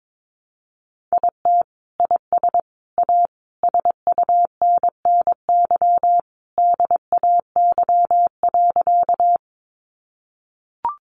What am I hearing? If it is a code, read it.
Morse code, heard as a keyed tone, about 22 wpm, 700 Hz. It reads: IT IS A SUNNY DAY.
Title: Additional Easy Phrases 22wpm